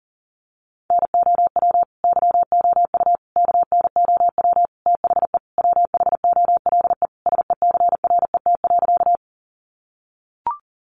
Morse code, heard as a keyed tone, about 40 wpm, 700 Hz. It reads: NOW YOU KNOW THE WHOLE SECRET.
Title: Sentences from Top 1200 Words 40wpm